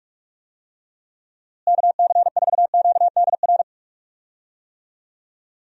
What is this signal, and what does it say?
Morse code, heard as a keyed tone, about 46 wpm, 700 Hz. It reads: KC4QDR
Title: USA Call Signs - 6 Characters - Repeated 2x - 45wpm